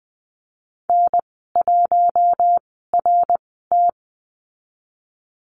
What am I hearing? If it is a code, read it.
Morse code, heard as a keyed tone, about 20 wpm, spaced as for 15 wpm, 700 Hz. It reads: N1RT
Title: 088 Lesson 15g - Number 5 - Call Signs 20wpm